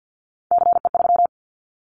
Morse code, heard as a keyed tone, about 50 wpm, 700 Hz. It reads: CE3
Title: Callsign Prefixes for Top 100 DXCC - 50wpm